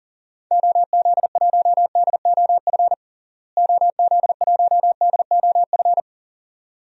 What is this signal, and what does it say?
Morse code, heard as a keyed tone, about 40 wpm, 700 Hz. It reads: OZ1DOF OZ1DOF
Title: DXCC-100 Call Signs - 6 Characters - Repeated 2x - 40wpm